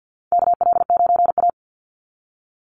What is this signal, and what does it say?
Morse code, heard as a keyed tone, about 50 wpm, 700 Hz. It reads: KL9A
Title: Call Signs Repeated 3x - 50wpm